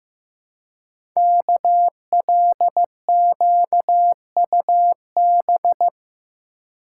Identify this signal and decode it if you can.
Morse code, heard as a keyed tone, about 15 wpm, 700 Hz. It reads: KLQUB
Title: Code Groups - 5 Letter:Numbers - 15wpm